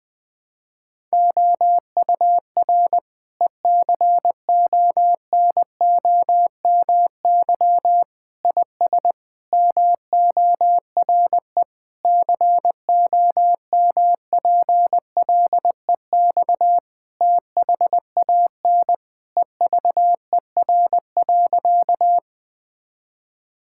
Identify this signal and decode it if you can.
Morse code, heard as a keyed tone, about 20 wpm, 700 Hz. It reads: OUR ECONOMY IS MORE COMPLEX THAN EVER.
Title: Sentences from Top 1000 Words - Review Part 6 - 20wpm